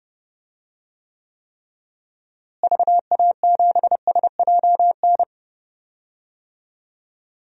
Morse code, heard as a keyed tone, about 30 wpm, 700 Hz. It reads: VA7SJN
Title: DXCC-100 Call Signs - 6 Characters - Repeated 3x - 30wpm